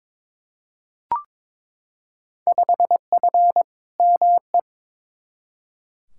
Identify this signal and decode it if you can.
Morse code, heard as a keyed tone, about 22 wpm, 700 Hz. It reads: 5F ME
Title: Field Day Exchanges Encore - 22wpm